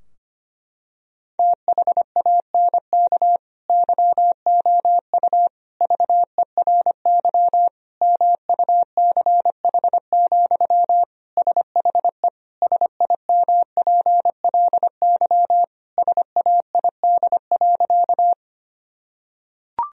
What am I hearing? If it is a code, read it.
Morse code, heard as a keyed tone, about 25 wpm, 700 Hz. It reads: THANK YOU VERY MUCH, SHE SIMPLY SAID.
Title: Sentences from Top 1000 Words - Review Part 4 - 25wpm